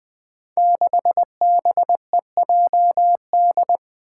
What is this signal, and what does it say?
Morse code, heard as a keyed tone, about 20 wpm, 700 Hz. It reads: 6BEJD
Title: Five Challenging Letter-Number Combinations 20wpm